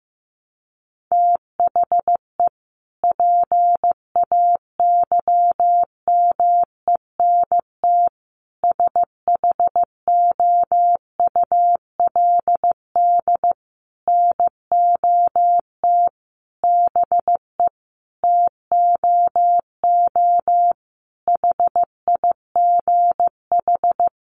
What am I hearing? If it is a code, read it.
Morse code, heard as a keyed tone, about 15 wpm, 700 Hz. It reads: THE PAYMENT SHOULD NOT BE TOO HIGH
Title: Sentences from Top 1200 Words 15wpm